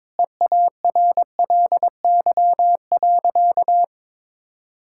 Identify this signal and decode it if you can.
Morse code, heard as a keyed tone, about 22 wpm, 700 Hz. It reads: EARLY.
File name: Sentences with 9-Word Phrases - 22wpm